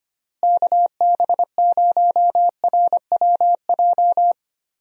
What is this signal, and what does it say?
Morse code, heard as a keyed tone, about 25 wpm, 700 Hz. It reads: KB0RWJ
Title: USA Call Signs - 6 Characters - Repeated 2x - 25wpm